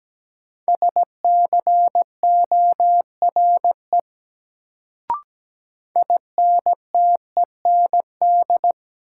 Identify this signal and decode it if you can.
Morse code, heard as a keyed tone, about 17 wpm, 700 Hz. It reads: SCORE INTEND
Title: Sentences from Top 1600 Words - New Word Drill Alt - 17wpm